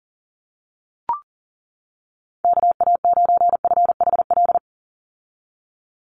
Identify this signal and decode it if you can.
Morse code, heard as a keyed tone, about 40 wpm, 700 Hz. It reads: KA9FHL
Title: USA Call Signs 40wpm